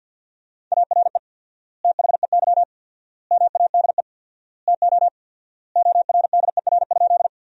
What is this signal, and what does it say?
Morse code, heard as a keyed tone, about 50 wpm, 700 Hz. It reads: ARE THEY MADE TO ORDER?
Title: 4x Word-Spacing - Sentences from Top 400 Words - 50wpm